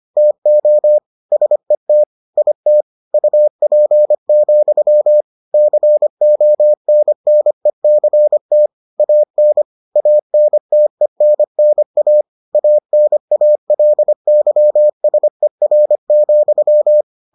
Morse code, heard as a keyed tone, about 25 wpm, 600 Hz. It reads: TO SET IT UP, CONNECT AN ANTENNA ANALYSER,